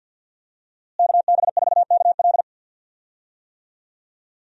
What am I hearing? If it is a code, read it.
Morse code, heard as a keyed tone, about 47 wpm, 700 Hz. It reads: KB4KL
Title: USA Call Signs - 5 Characters - Repeated 3x - 50wpm